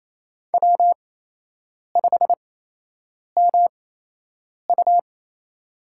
Morse code, heard as a keyed tone, about 28 wpm, spaced as for 8 wpm, 700 Hz. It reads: W5MU
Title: USA Call Signs - 8x Character Spacing - 28wpm